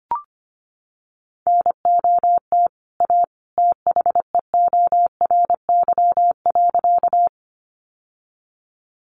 Morse code, heard as a keyed tone, about 25 wpm, 700 Hz. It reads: NOT A THEORY.
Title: Sentences from Top 600 Words 25wpm